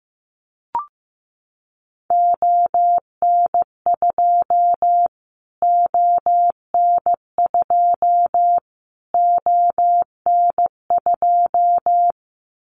Morse code, heard as a keyed tone, about 15 wpm, 700 Hz. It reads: ON2 ON2 ON2
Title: Callsign Prefixes for Top 25 DXCC - Repeated 3x - 15wpm